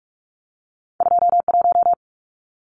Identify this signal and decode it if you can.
Morse code, heard as a keyed tone, about 46 wpm, 700 Hz. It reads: 21
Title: Numbers - 2-Digit 45wpm